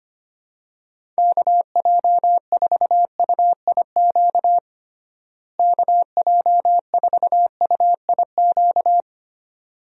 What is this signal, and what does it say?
Morse code, heard as a keyed tone, about 25 wpm, 700 Hz. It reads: KJ4UIQ KJ4UIQ
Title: USA Call Signs - 6 Characters - Repeated 2x - 25wpm